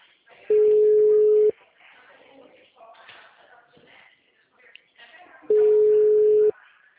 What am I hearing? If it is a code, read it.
Ringback tone (Europe, 425 Hz), with speech.